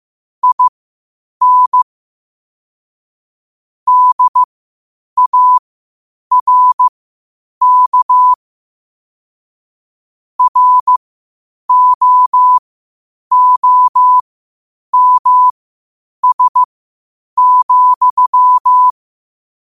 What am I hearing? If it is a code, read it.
Morse code, heard as a keyed tone, about 15 wpm, spaced as for 8 wpm, 1000 Hz. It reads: IN DARK ROOMS,
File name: qotdfiles / 8wpm